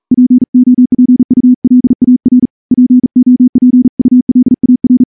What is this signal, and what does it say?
Morse code, heard as a keyed tone, about 40 wpm, 250 Hz. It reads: POPULAR POPULAR
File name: morse / morse-training-40WPM